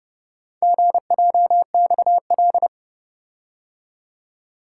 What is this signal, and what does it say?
Morse code, heard as a keyed tone, about 29 wpm, 700 Hz. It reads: GJXL